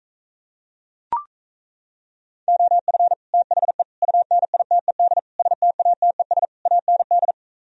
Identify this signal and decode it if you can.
Morse code, heard as a keyed tone, about 37 wpm, 700 Hz. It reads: OF THE UNITED STATES AND